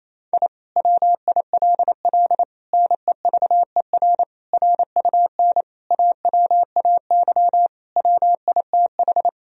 Morse code, heard as a keyed tone, about 28 wpm, 700 Hz. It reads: I WILL NEVER RUN AWAY WITH
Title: Sentences from Top 1000 Words - Review Part 1 - 28wpm